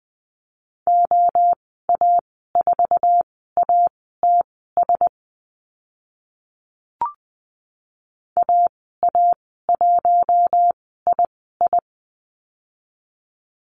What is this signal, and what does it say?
Morse code, heard as a keyed tone, about 20 wpm, spaced as for 14 wpm, 700 Hz. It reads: OA4ATS AA1II
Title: 062 Lesson 11g - Letter h - Call Signs 20wpm